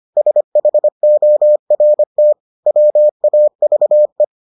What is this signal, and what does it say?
Morse code, heard as a keyed tone, about 25 wpm, 600 Hz. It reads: SHORT WAVE